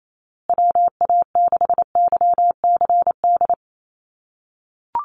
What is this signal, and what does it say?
Morse code, heard as a keyed tone, about 28 wpm, 700 Hz. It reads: WA6YCD
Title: USA Call Signs Repeated 2x 28wpm